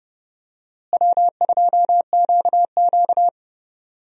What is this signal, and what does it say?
Morse code, heard as a keyed tone, about 31 wpm, 700 Hz. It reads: W2QQ
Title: USA Call Signs - 4 Characters - Repeated 3x - 30wpm